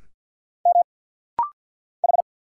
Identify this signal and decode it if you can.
Morse code, heard as a keyed tone, about 46 wpm, 700 Hz. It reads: M H
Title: Single Letters - Rapid-Fire 50wpm